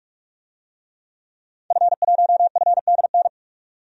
Morse code, heard as a keyed tone, about 47 wpm, 700 Hz. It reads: F1FDN
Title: DXCC-100 Call Signs - 5 Characters - 45wpm